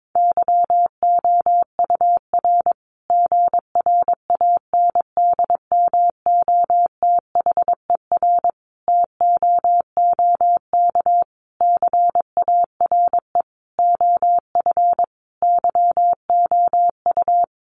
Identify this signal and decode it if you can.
Morse code, heard as a keyed tone, about 22 wpm, 700 Hz. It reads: YOUR GRANDMOTHER TOOK CARE OF YOU
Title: Sentences from Top 1200 Words 22wpm